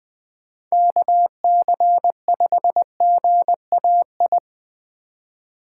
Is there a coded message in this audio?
Yes: Morse code from 0.7 to 4.4 s, heard as a keyed tone, about 20 wpm, 700 Hz. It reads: KC5GAI